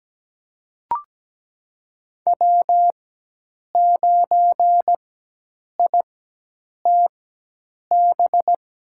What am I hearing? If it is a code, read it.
Morse code, heard as a keyed tone, about 17 wpm, spaced as for 8 wpm, 700 Hz. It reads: W9ITB
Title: USA Call Signs - 5 Characters - 4x Character Spacing - 17wpm